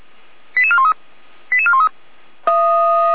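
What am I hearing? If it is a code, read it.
Five-tone selective call (ZVEI), heard as quick runs of five tones, tones of about 70 ms. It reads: 88413 88413
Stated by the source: ZVEI five-tone selective call received by SDR.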